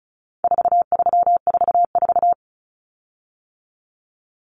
Morse code, heard as a keyed tone, about 34 wpm, 700 Hz. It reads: 4344
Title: Common Cut Numbers - 4-Digit 35wpm